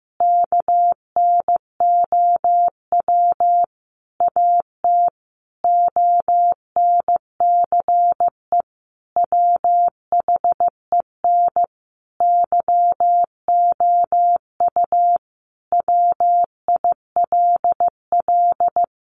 Morse code, heard as a keyed tone, about 15 wpm, 700 Hz. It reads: KNOW AT ONCE WHEN YOU WILL
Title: Sentences from Top 300 Words 15wpm